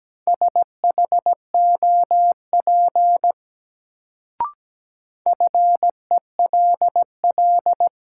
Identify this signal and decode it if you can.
Morse code, heard as a keyed tone, about 17 wpm, 700 Hz. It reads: SHOP FELL
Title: Sentences from Top 1600 Words - New Word Drill - 17wpm